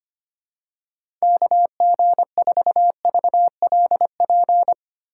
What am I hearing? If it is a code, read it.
Morse code, heard as a keyed tone, about 25 wpm, 700 Hz. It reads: KG4VLP
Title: USA Call Signs - 6 Characters - 8x Character Spacing - 25wpm